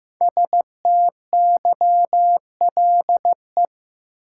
Morse code, heard as a keyed tone, about 15 wpm, 700 Hz. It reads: STYLE
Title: Sentences from Top 800 Words - New Word Drill - 15wpm